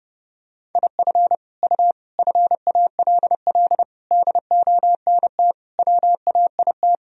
Morse code, heard as a keyed tone, about 30 wpm, 700 Hz. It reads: IF U FALL DONT WAIT
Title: Sentences from Top 500 Words and Abbreviations 30wpm